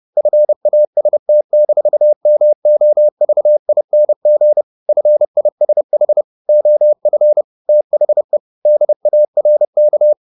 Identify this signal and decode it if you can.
Morse code, heard as a keyed tone, about 30 wpm, 600 Hz. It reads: FAST-MOVING FISH OF THE DARK